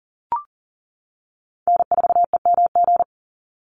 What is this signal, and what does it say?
Morse code, heard as a keyed tone, about 40 wpm, 700 Hz. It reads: N4EMG
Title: NA SOTA Chaser Callsigns 40wpm